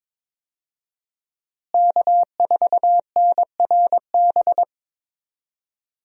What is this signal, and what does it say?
Morse code, heard as a keyed tone, about 22 wpm, 700 Hz. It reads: K4NRB
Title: USA Call Signs - 5 Characters - Repeated 3x - 22wpm